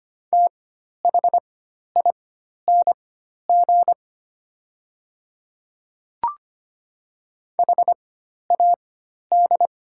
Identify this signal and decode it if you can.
Morse code, heard as a keyed tone, about 25 wpm, spaced as for 12 wpm, 700 Hz. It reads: THING HAD
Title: 4x Character Spacing - Top 100 Words - 25wpm